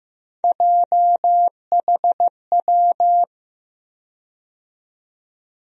Morse code, heard as a keyed tone, about 15 wpm, 700 Hz. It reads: JHW
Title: USA Call Sign Postfixes - 3 Letters 15wpm